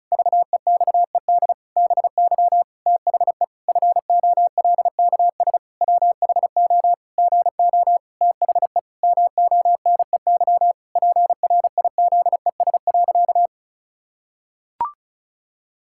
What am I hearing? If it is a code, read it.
Morse code, heard as a keyed tone, about 36 wpm, 700 Hz. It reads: VEXED BY THE FOLKS WHO GO THE MONEY PRIZES.
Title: Pangram Challenge v2 - 35wpm